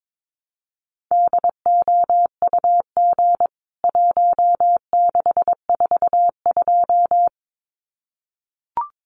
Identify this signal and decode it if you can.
Morse code, heard as a keyed tone, about 22 wpm, 700 Hz. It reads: DOUG 1642
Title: CWT Exchange 22wpm